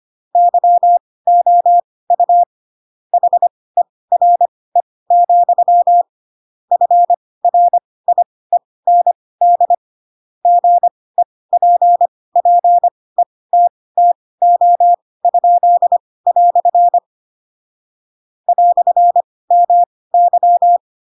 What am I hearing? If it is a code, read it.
Morse code, heard as a keyed tone, about 25 wpm, spaced as for 18 wpm, 700 Hz. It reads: YOU HERE, FRIEND GEPPETTO?" "MY
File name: Pinocchio / Pinocchio ch2 12 25 ik1hge